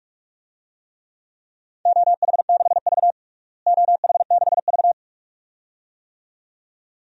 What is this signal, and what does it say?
Morse code, heard as a keyed tone, about 46 wpm, 700 Hz. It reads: OH6V OH6V
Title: DXCC-100 Call Signs - 4 Characters - Repeated 2x - 45wpm